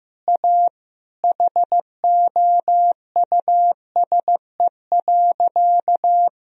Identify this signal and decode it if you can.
Morse code, heard as a keyed tone, about 15 wpm, 700 Hz. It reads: A HOUSE.
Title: Sentences from Top 1000 Words - Review Part 4 - 15wpm